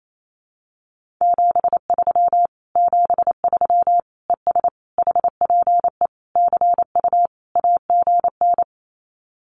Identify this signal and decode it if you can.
Morse code, heard as a keyed tone, about 28 wpm, 700 Hz. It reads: 73 73 ES HPE CU AGN
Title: Common QSO Phrases Encore 28wpm